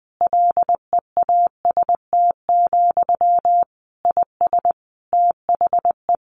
Morse code, heard as a keyed tone, about 20 wpm, 700 Hz. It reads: LEAST, IS THE
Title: Sentences from Top 1500 Words 20wpm